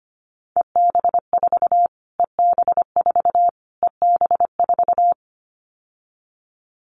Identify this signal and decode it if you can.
Morse code, heard as a keyed tone, about 25 wpm, 700 Hz. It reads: EB4 EB4 EB4
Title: Callsign Prefixes for Top 25 DXCC - Repeated 3x - 25wpm